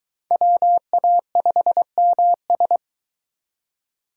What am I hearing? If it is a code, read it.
Morse code, heard as a keyed tone, about 22 wpm, 700 Hz. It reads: WA5MS